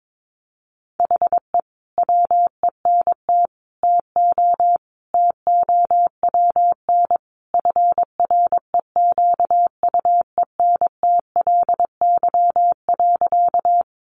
Morse code, heard as a keyed tone, about 22 wpm, 700 Hz. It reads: HE WENT TO TOWN FREQUENTLY.
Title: Sentences from Top 1300 Words 22wpm